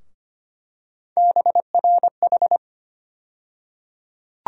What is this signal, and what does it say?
Morse code, heard as a keyed tone, about 25 wpm, 700 Hz. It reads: BRH
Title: USA Call Sign Postfixes - 3 Letters 25wpm